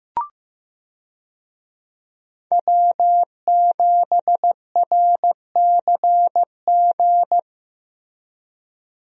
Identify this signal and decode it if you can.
Morse code, heard as a keyed tone, about 15 wpm, 700 Hz. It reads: W7RCG
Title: USA Call Signs - 5 Characters - Repeated 2x - 15wpm